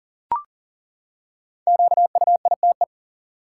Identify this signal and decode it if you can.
Morse code, heard as a keyed tone, about 40 wpm, 700 Hz. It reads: QUITE
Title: Five Letter Words - 40wpm